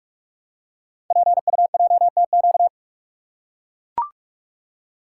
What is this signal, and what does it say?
Morse code, heard as a keyed tone, about 44 wpm, 700 Hz. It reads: PUJTQ
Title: Code Groups - 5 Letter:Numbers - 45wpm